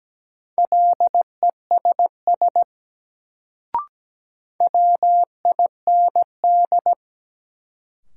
Morse code, heard as a keyed tone, about 17 wpm, 700 Hz. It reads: LESS WIND
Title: Sentences from Top 400 Words - New Word Drill - 17wpm